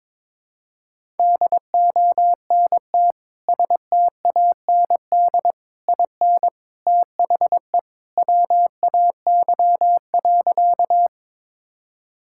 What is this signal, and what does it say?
Morse code, heard as a keyed tone, about 21 wpm, 700 Hz. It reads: DONT STAND IN THE WAY.
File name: practice / 22wpm / Sentences with 3-Word Phrases - 22wpm